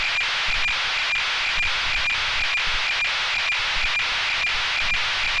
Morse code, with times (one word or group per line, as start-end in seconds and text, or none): none